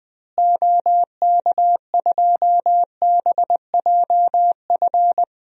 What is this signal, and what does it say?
Morse code, heard as a keyed tone, about 20 wpm, 700 Hz. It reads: OK2BJF
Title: DXCC-100 Call Signs - 6 Characters - Repeated 3x - 20wpm